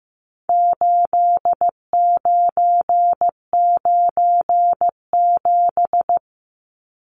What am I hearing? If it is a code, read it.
Morse code, heard as a keyed tone, about 15 wpm, 700 Hz. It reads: 8997
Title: Numbers - 4-Digit 15wpm